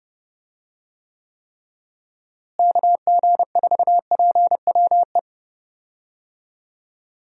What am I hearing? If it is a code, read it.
Morse code, heard as a keyed tone, about 30 wpm, 700 Hz. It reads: KG4PWE
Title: USA Call Signs - 6 Characters - Repeated 3x - 30wpm